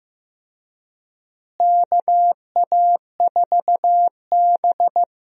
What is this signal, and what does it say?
Morse code, heard as a keyed tone, about 15 wpm, 700 Hz. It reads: KA4B
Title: USA Call Signs - 4 Characters - Repeated 3x - 15wpm